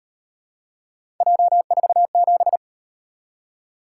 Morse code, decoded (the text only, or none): J47